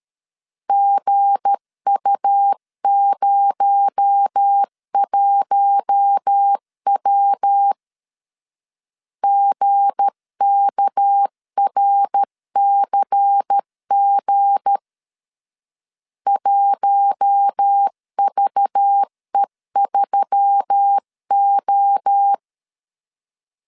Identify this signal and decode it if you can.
Morse code, heard as a keyed tone, about 13 wpm, 800 Hz. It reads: GU01W GKRCG 1VE3O
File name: LEZIONI / 7 GF9Z / 7 G 12wpm 2055